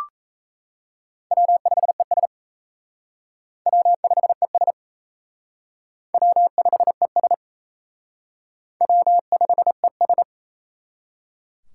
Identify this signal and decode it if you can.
Morse code, heard as a keyed tone, about 34 wpm, 700 Hz. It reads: W5ES W5ES W5ES W5ES
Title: Speed Racing - USA 1x2 Call Signs 28wpm